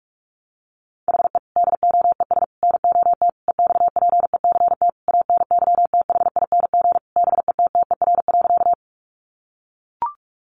Morse code, heard as a keyed tone, about 44 wpm, 700 Hz. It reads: HE DOES NOT EXPECT ANYTHING BETTER.
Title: Sentences from Top 1100 Words 45wpm